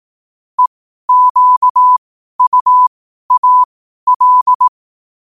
Morse code, heard as a keyed tone, about 18 wpm, spaced as for 12 wpm, 1000 Hz. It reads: EQUAL
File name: qotdfiles / 12wpm